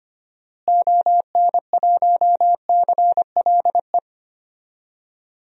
Morse code, heard as a keyed tone, about 25 wpm, 700 Hz. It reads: ON1CLE
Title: DXCC-100 Call Signs - 6 Characters - Repeated 2x - 25wpm